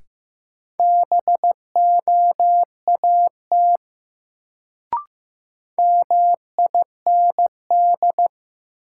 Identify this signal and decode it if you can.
Morse code, heard as a keyed tone, about 15 wpm, 700 Hz. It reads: BOAT MIND
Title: Sentences from Top 500 Words - New Word Drill - 15wpm